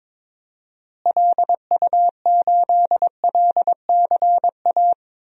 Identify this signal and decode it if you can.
Morse code, heard as a keyed tone, about 21 wpm, 700 Hz. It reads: LU8LCA
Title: DXCC-100 Call Signs - 6 Characters - Repeated 2x - 22wpm